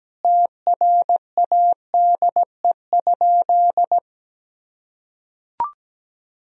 Morse code, heard as a keyed tone, about 17 wpm, 700 Hz. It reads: TRADE?
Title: Sentences from Top 1000 Words - Review Part 4 - 17wpm